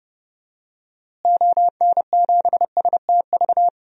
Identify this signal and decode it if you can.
Morse code, heard as a keyed tone, about 30 wpm, 700 Hz. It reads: ON7STV
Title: DXCC-100 Call Signs - 6 Characters - Repeated 2x - 30wpm